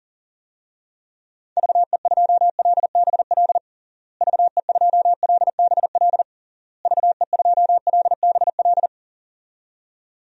Morse code, heard as a keyed tone, about 40 wpm, 700 Hz. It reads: VE2LBL VE2LBL VE2LBL
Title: DXCC-100 Call Signs - 6 Characters - Repeated 3x - 40wpm